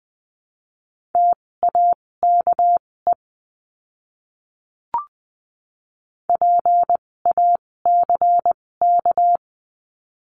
Morse code, heard as a keyed tone, about 20 wpm, spaced as for 16 wpm, 700 Hz. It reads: TAKE PACK
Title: 256 Lesson 42a - Words 12-20wpm